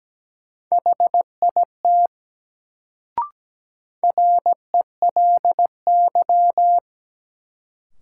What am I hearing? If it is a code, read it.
Morse code, heard as a keyed tone, about 17 wpm, 700 Hz. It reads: HIT RELY
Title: Sentences from Top 1200 Words - New Word Drill - 17wpm